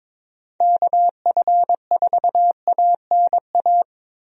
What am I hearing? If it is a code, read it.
Morse code, heard as a keyed tone, about 22 wpm, 700 Hz. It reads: KF4ANA
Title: USA Call Signs - 6 Characters - Repeated 3x - 22wpm